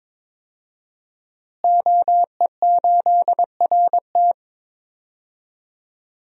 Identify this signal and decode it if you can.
Morse code, heard as a keyed tone, about 21 wpm, 700 Hz. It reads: OE8RT
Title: DXCC-100 Call Signs - 5 Characters - Repeated 3x - 22wpm